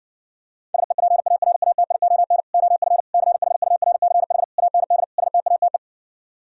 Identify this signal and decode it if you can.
Morse code, heard as a keyed tone, about 60 wpm, 700 Hz. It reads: SEPARATION OF CHURCH AND STATE